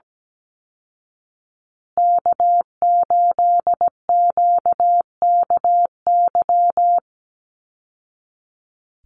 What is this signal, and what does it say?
Morse code, heard as a keyed tone, about 17 wpm, 700 Hz. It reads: K8QKY
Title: Call Signs Repeated 2x - 17wpm